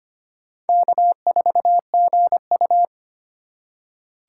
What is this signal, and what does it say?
Morse code, heard as a keyed tone, about 25 wpm, 700 Hz. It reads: K4GU
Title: USA Call Signs - 4x Character Spacing - 25wpm